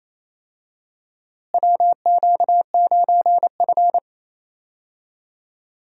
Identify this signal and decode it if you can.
Morse code, heard as a keyed tone, about 27 wpm, 700 Hz. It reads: WQ9F
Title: USA Call Signs - 4 Characters - Repeated 2x - 28wpm